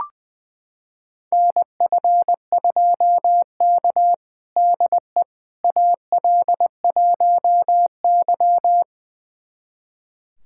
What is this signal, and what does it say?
Morse code, heard as a keyed tone, about 20 wpm, 700 Hz. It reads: NF2K DE AL1Y
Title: USA Callsign DE Callsign - 4 Characters - 20wpm